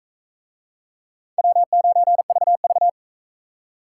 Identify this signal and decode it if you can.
Morse code, heard as a keyed tone, about 39 wpm, 700 Hz. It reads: W9VV